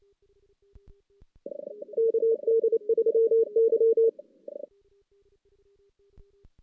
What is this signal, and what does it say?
Morse code, heard as a keyed tone, about 28 wpm, 450 Hz. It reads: KD3Y
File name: 2016 JAN CW NAQP / 2016 Jan naqp k5ot - 0-5